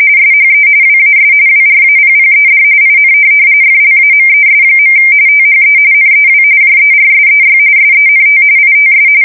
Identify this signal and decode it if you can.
RTTY, about 45.45 baud. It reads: SHE WAS WRAPPED IN FLAMES. THE NEW JERSEY SENT ITS
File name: fsk170 / USS-New-Jersey